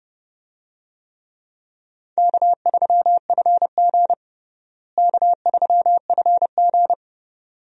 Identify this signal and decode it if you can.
Morse code, heard as a keyed tone, about 30 wpm, 700 Hz. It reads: K3FG K3FG
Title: USA Call Signs - 4 Characters - Repeated 2x - 30wpm